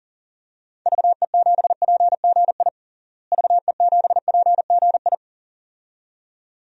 Morse code, heard as a keyed tone, about 40 wpm, 700 Hz. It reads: VE7PGI VE7PGI
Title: DXCC-100 Call Signs - 6 Characters - Repeated 2x - 40wpm